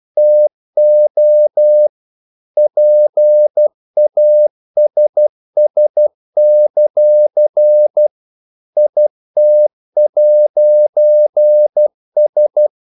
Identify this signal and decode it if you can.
Morse code, heard as a keyed tone, about 12 wpm, 600 Hz. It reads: TO PASS; IT'S
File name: CW / Practice / EnglishQuotations0112